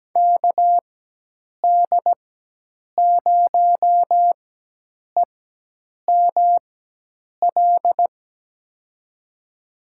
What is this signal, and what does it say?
Morse code, heard as a keyed tone, about 17 wpm, spaced as for 8 wpm, 700 Hz. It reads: KD0EML
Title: USA Call Signs - 4x Character Spacing - 17wpm